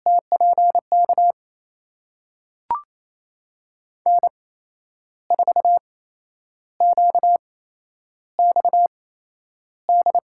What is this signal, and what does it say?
Morse code, heard as a keyed tone, about 28 wpm, spaced as for 10 wpm, 700 Hz. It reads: TPK N4QXD
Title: USA Call Signs - 5 Characters - 8x Character Spacing - 28wpm